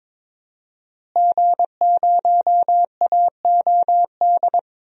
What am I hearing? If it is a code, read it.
Morse code, heard as a keyed tone, about 22 wpm, 700 Hz. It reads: G0AOD
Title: DXCC-100 Call Signs - 5 Characters - Repeated 2x - 22wpm